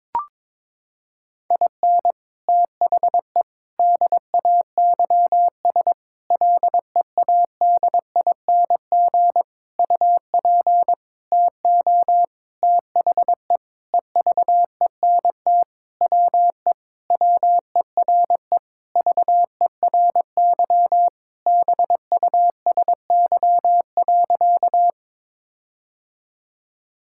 Morse code, heard as a keyed tone, about 22 wpm, 700 Hz. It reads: IN THE DAYS LEADING UP TO THE EVENT WE WERE VERY BUSY.
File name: Sentences with 7-Word Phrases - 22wpm